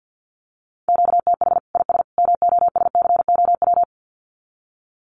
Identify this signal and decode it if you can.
Morse code, heard as a keyed tone, about 49 wpm, 700 Hz. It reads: QTH IS MOSCOW
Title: Common QSO Phrases 50wpm